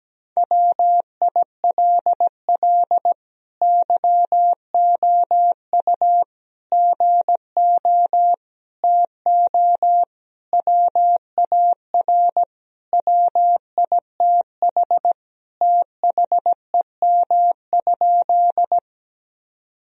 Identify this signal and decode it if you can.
Morse code, heard as a keyed tone, about 17 wpm, 700 Hz. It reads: WILL YOU GO TO WAR WITH THEM?
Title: Sentences from Top 1000 Words - Review Part 2 - 17wpm